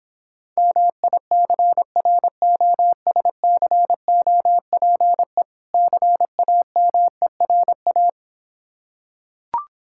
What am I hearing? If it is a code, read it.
Morse code, heard as a keyed tone, about 26 wpm, 700 Hz. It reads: MICROSCOPE CAMERA